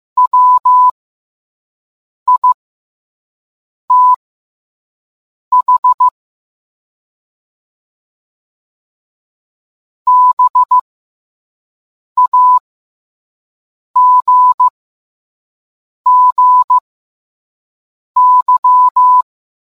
Morse code, heard as a keyed tone, about 15 wpm, spaced as for 5 wpm, 1000 Hz. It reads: WITH BAGGY